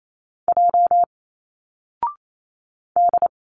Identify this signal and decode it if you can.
Morse code, heard as a keyed tone, about 28 wpm, 700 Hz. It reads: J D
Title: Single Letters 28wpm